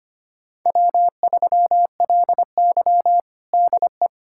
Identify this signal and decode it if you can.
Morse code, heard as a keyed tone, about 25 wpm, 700 Hz. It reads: W3LY DE
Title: USA Callsign DE Callsign - 4 Characters - 25wpm